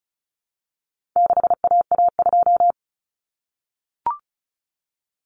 Morse code, heard as a keyed tone, about 34 wpm, 700 Hz. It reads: 6AA2
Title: Common Cut Numbers - 4-Digit 35wpm